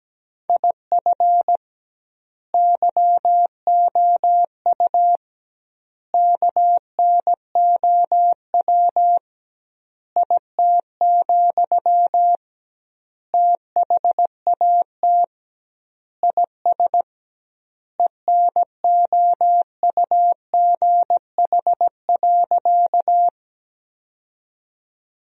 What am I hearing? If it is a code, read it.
Morse code, heard as a keyed tone, about 17 wpm, 700 Hz. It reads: IF YOU KNOW IT, THAT IS ENOUGH.
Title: 2x Word-Spacing - Sentences from Top 400 Words - 17wpm